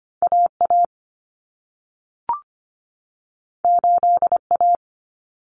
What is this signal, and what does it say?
Morse code, heard as a keyed tone, about 25 wpm, 700 Hz. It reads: AA 8A
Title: Common Cut Numbers - 2-Digit 25wpm